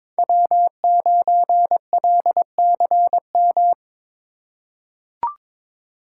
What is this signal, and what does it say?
Morse code, heard as a keyed tone, about 21 wpm, 700 Hz. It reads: W9LCM
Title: USA Call Signs - 2x Character Spacing - 22wpm